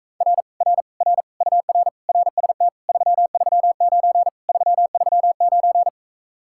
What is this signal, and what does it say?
Morse code, heard as a keyed tone, about 45 wpm, 700 Hz. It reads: R R R UR RST 339 339